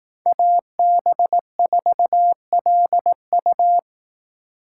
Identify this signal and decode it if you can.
Morse code, heard as a keyed tone, about 18 wpm, 700 Hz. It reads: AB4LU
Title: Speed Racing - USA 2x2 Call Signs 15wpm